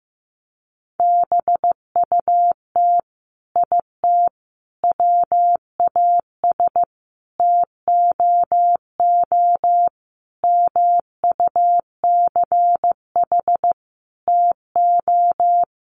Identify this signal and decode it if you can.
Morse code, heard as a keyed tone, about 15 wpm, 700 Hz. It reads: BUT IT WAS TOO MUCH TO
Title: Sentences from Top 1100 Words 15wpm